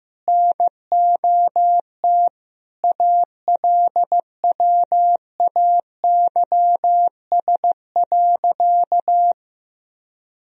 Morse code, heard as a keyed tone, about 15 wpm, 700 Hz. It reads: NOT ALWAYS.
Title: Sentences from Top 1000 Words - Review Part 4 - 15wpm